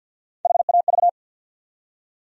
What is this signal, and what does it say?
Morse code, heard as a keyed tone, about 47 wpm, 700 Hz. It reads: HAV
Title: USA Call Sign Postfixes - 3 Letters 50wpm